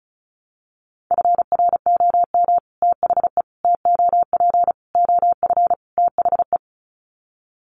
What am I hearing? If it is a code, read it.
Morse code, heard as a keyed tone, about 34 wpm, 700 Hz. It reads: FROM THE TOP OF THE
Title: Sets of 5 Words Encore 35wpm